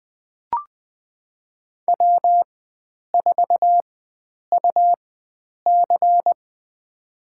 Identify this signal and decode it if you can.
Morse code, heard as a keyed tone, about 20 wpm, spaced as for 9 wpm, 700 Hz. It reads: W4UC
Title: USA Call Signs - 4 Characters - 4x Character Spacing - 20wpm